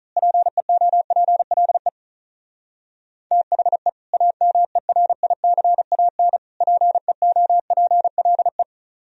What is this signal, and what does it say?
Morse code, heard as a keyed tone, about 34 wpm, 700 Hz. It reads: PEOPLE THE AMERICAN PEOPLE
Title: Speed-Racing - Sets of 3 Words 35wpm